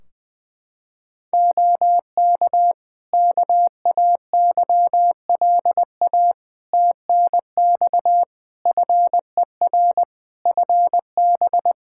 Morse code, heard as a keyed tone, about 20 wpm, 700 Hz. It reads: OK KAYLA TNX FER FB
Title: K1USN SST QSO Practice 20wpm